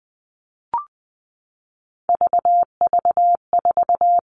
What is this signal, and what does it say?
Morse code, heard as a keyed tone, about 20 wpm, 700 Hz. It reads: VV4
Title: All Cut Numbers - 3-Digit 20wpm